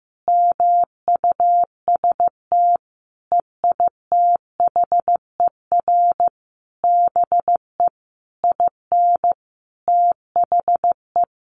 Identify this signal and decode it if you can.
Morse code, heard as a keyed tone, about 15 wpm, 700 Hz. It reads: MUST EITHER BE IN THE
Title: Sentences from Top 1400 Words 15wpm